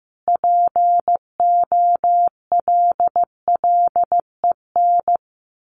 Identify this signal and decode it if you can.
Morse code, heard as a keyed tone, about 15 wpm, 700 Hz. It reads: POLLEN
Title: A Sound Of Thunder 15wpm